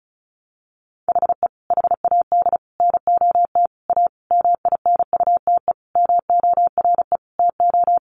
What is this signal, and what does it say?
Morse code, heard as a keyed tone, about 34 wpm, 700 Hz. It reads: HE HAD NOT A MINUTE MORE TO
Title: Sentences from Top 1100 Words 35wpm